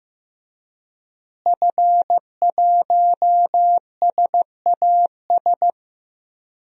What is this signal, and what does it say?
Morse code, heard as a keyed tone, about 16 wpm, 700 Hz. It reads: F1SAS
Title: DXCC-100 Call Signs - 5 Characters - Repeated 3x - 15wpm